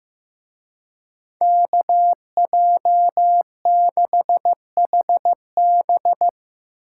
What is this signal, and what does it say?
Morse code, heard as a keyed tone, about 16 wpm, 700 Hz. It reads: KJ6HB
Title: Speed Racing - USA Call Signs 15wpm